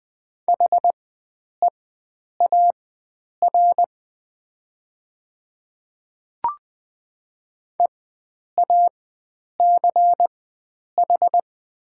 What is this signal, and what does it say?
Morse code, heard as a keyed tone, about 20 wpm, spaced as for 9 wpm, 700 Hz. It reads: HEAR EACH
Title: 4x Character Spacing - Top 500 Words - 20wpm